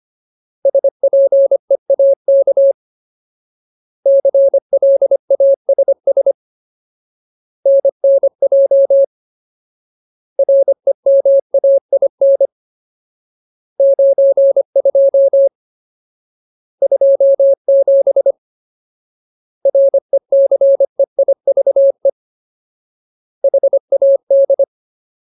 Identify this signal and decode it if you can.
Morse code, heard as a keyed tone, about 25 wpm, 550 Hz. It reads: SPEAK CLASS NNJ REMAIN 92 27 RECEIVE HAD
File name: Ultimate Word List / 25WPM Ultimate